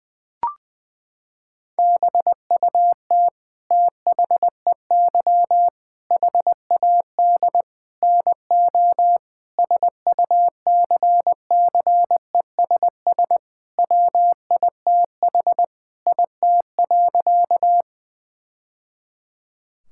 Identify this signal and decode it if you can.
Morse code, heard as a keyed tone, about 20 wpm, 700 Hz. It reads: BUT THEY HAD NO SUCCESS WITH IT.
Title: Sentences from Top 1000 Words - Review Part 6 - 20wpm